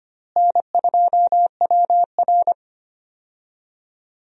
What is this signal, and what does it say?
Morse code, heard as a keyed tone, about 25 wpm, 700 Hz. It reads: N2WR
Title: USA Call Signs - 4 Characters - Repeated 3x - 25wpm